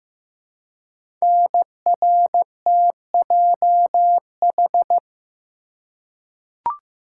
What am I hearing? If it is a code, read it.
Morse code, heard as a keyed tone, about 15 wpm, 700 Hz. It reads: NRTJH
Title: Code Groups - 5 Letters - 15wpm